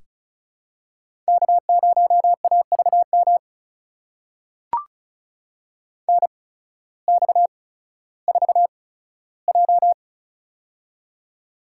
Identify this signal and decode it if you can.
Morse code, heard as a keyed tone, about 36 wpm, spaced as for 16 wpm, 700 Hz. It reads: K0AVM NX4J
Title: USA Call Signs - 8x Character Spacing - 35wpm